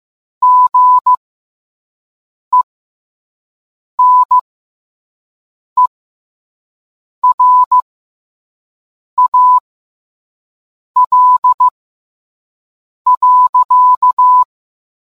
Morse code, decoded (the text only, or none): GENERAL.